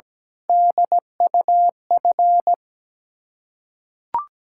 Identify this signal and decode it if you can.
Morse code, heard as a keyed tone, about 17 wpm, 690 Hz. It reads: DUF
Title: Ontario QSO Party Exchange 17wpm